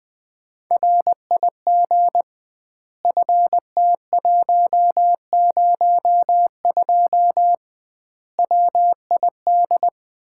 Morse code, heard as a keyed tone, about 20 wpm, 700 Hz. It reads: RIG FT102 WID
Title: Common QSO Phrases - 2x Word-Spacing - 20wpm